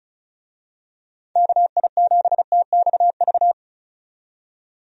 Morse code, heard as a keyed tone, about 36 wpm, 700 Hz. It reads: KI7TXV
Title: USA Call Signs - 6 Characters - 35wpm